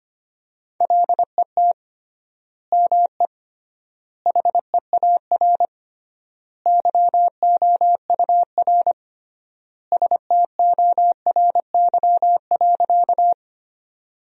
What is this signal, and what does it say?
Morse code, heard as a keyed tone, about 25 wpm, 700 Hz. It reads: LET ME HEAR YOUR STORY.
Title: 3x Word-Spacing - Sentences from Top 300 Words - 25wpm